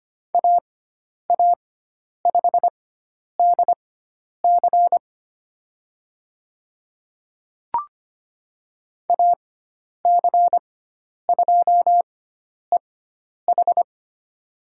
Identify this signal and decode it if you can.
Morse code, heard as a keyed tone, about 25 wpm, spaced as for 10 wpm, 700 Hz. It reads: AA5DC AC2EH
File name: Lesson 16g - Letter c - Call Signs 25wpm